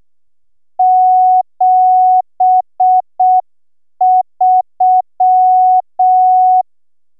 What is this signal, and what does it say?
Morse code, heard as a keyed tone, about 6 wpm, 740 Hz. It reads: 73